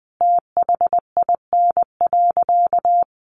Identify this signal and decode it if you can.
Morse code, heard as a keyed tone, about 20 wpm, 700 Hz. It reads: THIN.
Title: Sentences from Top 1200 Words 20wpm